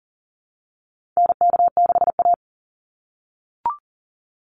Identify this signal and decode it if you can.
Morse code, heard as a keyed tone, about 40 wpm, 700 Hz. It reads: NK6A
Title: NA SOTA Chaser Callsigns 40wpm